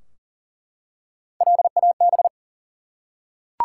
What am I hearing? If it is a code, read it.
Morse code, heard as a keyed tone, about 42 wpm, 700 Hz. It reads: LAB
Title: Three Letter Words - 40wpm